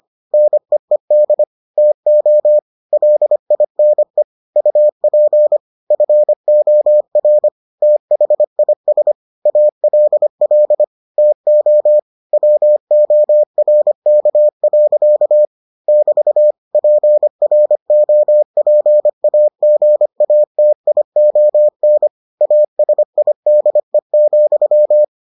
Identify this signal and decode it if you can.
Morse code, heard as a keyed tone, about 25 wpm, 600 Hz. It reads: NEED TO LINE UP FOR THIS ALL TO WORK. <BT> PROPAGATION ASIDE,